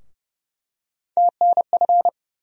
Morse code, heard as a keyed tone, about 30 wpm, 700 Hz. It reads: TNF
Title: USA Call Sign Postfixes - 3 Letters 30wpm